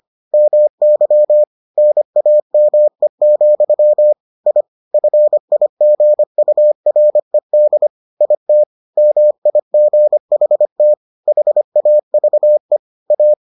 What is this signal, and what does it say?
Morse code, heard as a keyed tone, about 25 wpm, 600 Hz. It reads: MY NAME, I FIGURED IT MIGHT HAVE A